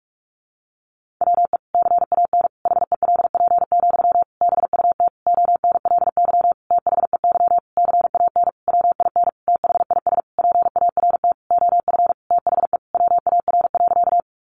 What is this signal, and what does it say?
Morse code, heard as a keyed tone, about 44 wpm, 700 Hz. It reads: WE CAN HELP, BUT ONLY THEY CAN WIN THIS PART OF THE WAR.
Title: Sentences from Top 1500 Words 45wpm